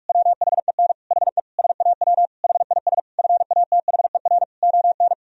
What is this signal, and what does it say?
Morse code, heard as a keyed tone, about 46 wpm, 700 Hz. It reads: WHEN HE SAW HIS FATHER ON
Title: Sentences from Top 500 Words and Abbreviations 45wpm